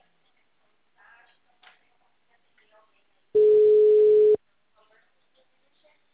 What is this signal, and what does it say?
Ringback tone (Europe, 425 Hz), with speech.